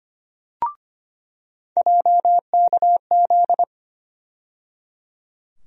Morse code, heard as a keyed tone, about 25 wpm, 700 Hz. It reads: JKZ
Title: USA Call Sign Postfixes - 3 Letters 25wpm